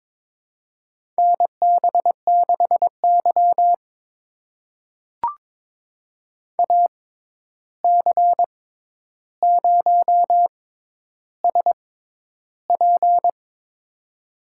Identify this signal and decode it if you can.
Morse code, heard as a keyed tone, about 22 wpm, spaced as for 11 wpm, 700 Hz. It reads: NB6Y AC0SP
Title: USA Call Signs - 6x Character Spacing - 22wpm